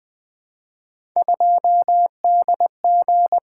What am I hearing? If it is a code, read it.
Morse code, heard as a keyed tone, about 20 wpm, 700 Hz. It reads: 2DG
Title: Three Letter-Number Combinations - 20wpm